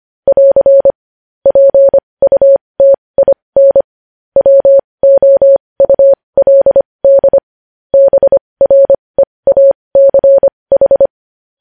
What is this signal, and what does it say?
Morse code, heard as a keyed tone, about 25 wpm, spaced as for 20 wpm, 550 Hz. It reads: <AR> PUTIN WOULD BREACH
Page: news-headlines-in-morse-code-at-20-wpm